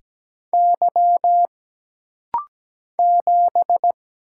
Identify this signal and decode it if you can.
Morse code, heard as a keyed tone, about 17 wpm, 700 Hz. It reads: Y 7
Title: Single Letter-Number - Rapid-Fire 17wpm